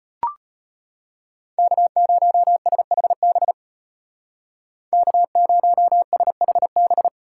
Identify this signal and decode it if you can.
Morse code, heard as a keyed tone, about 30 wpm, 700 Hz. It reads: K0SHB K0SHB
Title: Speed Racing - USA 1x3 Call Signs 25wpm